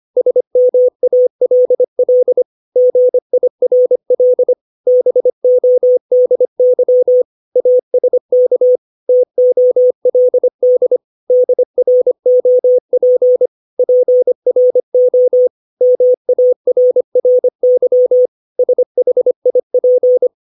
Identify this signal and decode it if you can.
Morse code, heard as a keyed tone, about 25 wpm, 500 Hz. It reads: SMALL GIRL BODY ASK TOLD DROP PRO MARRY SHIP
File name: 632 words / 632 Words 25WPM